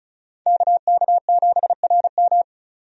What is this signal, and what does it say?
Morse code, heard as a keyed tone, about 36 wpm, 700 Hz. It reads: KK7RM
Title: USA Call Signs - 5 Characters - 35wpm